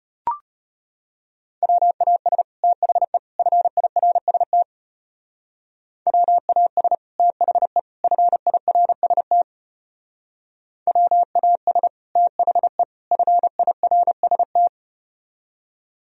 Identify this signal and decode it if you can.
Morse code, heard as a keyed tone, about 33 wpm, 700 Hz. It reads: WAS THE FIRST WAS THE FIRST WAS THE FIRST